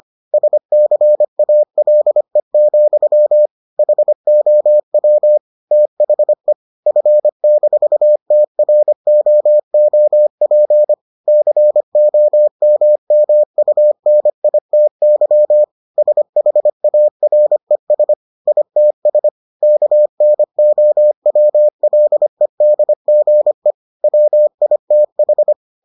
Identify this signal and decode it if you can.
Morse code, heard as a keyed tone, about 25 wpm, 600 Hz. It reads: SCALE, HOW THE F-TROOP COMMUNITY SHARES ITS KNOWLEDGE WITH